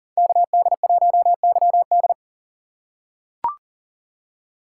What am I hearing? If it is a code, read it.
Morse code, heard as a keyed tone, about 39 wpm, 690 Hz. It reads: KD1YD
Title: USA Call Signs - 2x Character Spacing - 40wpm